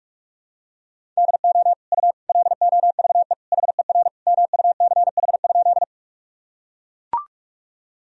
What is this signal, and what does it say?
Morse code, heard as a keyed tone, about 41 wpm, 700 Hz. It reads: DO U LOVE HER MUCH?
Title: Sentences from Top 400 Words and Abbreviations 45wpm